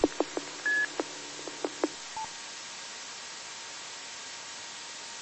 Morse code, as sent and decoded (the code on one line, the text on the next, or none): none